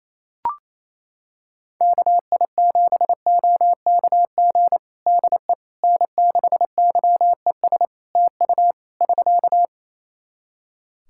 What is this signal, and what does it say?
Morse code, heard as a keyed tone, about 28 wpm, 700 Hz. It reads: KI7OKG DE N6YES TU <SK>
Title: Practice QSO - Exchange 4 - Ending - 28wpm